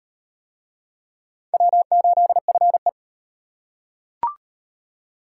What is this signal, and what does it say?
Morse code, heard as a keyed tone, about 37 wpm, 700 Hz. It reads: W8FE